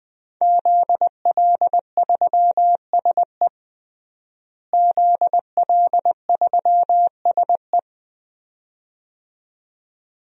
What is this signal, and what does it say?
Morse code, heard as a keyed tone, about 20 wpm, 700 Hz. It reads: ZL3SE ZL3SE
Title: DXCC-100 Call Signs - 5 Characters - Repeated 2x - 20wpm